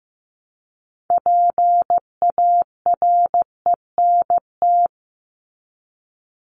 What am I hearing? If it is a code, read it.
Morse code, heard as a keyed tone, about 15 wpm, 700 Hz. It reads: PARENT
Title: ES:S Suffix - Top 100 Words with Base Words 15wpm